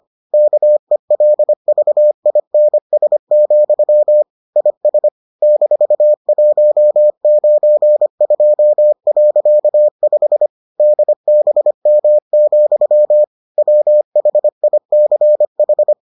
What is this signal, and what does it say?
Morse code, heard as a keyed tone, about 25 wpm, 600 Hz. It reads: KELVINS, IS -192.5 DBM, WHICH